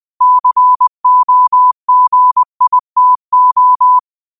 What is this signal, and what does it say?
Morse code, heard as a keyed tone, about 20 wpm, 1000 Hz. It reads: COGITO